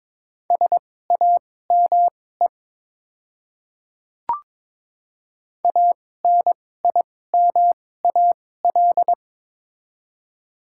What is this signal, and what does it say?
Morse code, heard as a keyed tone, about 22 wpm, spaced as for 16 wpm, 690 Hz. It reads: SAME ANIMAL
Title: 2x Character Spacing - Top 300 Words - 22wpm